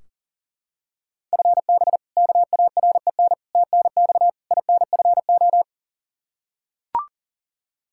Morse code, heard as a keyed tone, about 40 wpm, 700 Hz. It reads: FB KAREN TNX INFO
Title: K1USN SST QSO Practice 40wpm